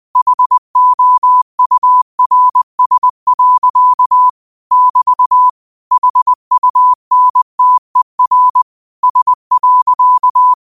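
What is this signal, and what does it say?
Morse code, heard as a keyed tone, about 20 wpm, 1000 Hz. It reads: HOURS. <BT> HUNTER S.
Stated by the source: Quote of the day sent in Morse at 20 words per minute.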